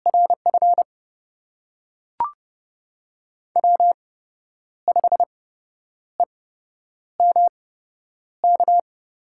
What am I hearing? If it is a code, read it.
Morse code, heard as a keyed tone, about 31 wpm, spaced as for 10 wpm, 700 Hz. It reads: RF W5EMK
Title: USA Call Signs - 5 Characters - 8x Character Spacing - 30wpm